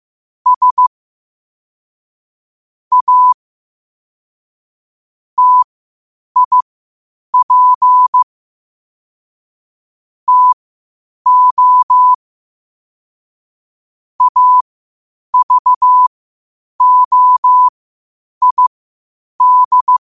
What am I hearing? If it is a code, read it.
Morse code, heard as a keyed tone, about 15 wpm, spaced as for 8 wpm, 1000 Hz. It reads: S A TIP TO AVOID
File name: qotdfiles / 8wpm